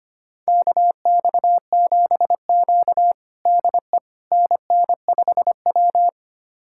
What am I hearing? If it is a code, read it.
Morse code, heard as a keyed tone, about 25 wpm, 700 Hz. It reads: KX7Q DE NN5W
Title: USA Callsign DE Callsign - 4 Characters - 25wpm